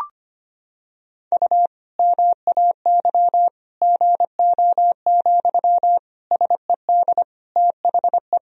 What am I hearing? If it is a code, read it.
Morse code, heard as a keyed tone, about 25 wpm, 700 Hz. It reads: U MAY GO, SED THE
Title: Sentences from Top 400 Words and Abbreviations 25wpm